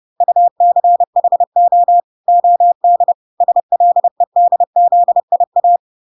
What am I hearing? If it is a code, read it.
Morse code, heard as a keyed tone, about 30 wpm, 700 Hz. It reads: UCHO OD SLEDZIA